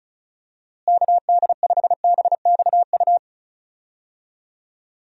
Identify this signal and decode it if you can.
Morse code, heard as a keyed tone, about 34 wpm, 700 Hz. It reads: KD5BXU
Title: USA Call Signs - 6 Characters - Repeated 3x - 35wpm